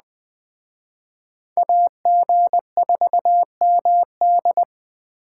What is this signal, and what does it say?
Morse code, heard as a keyed tone, about 20 wpm, 700 Hz. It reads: AG4MD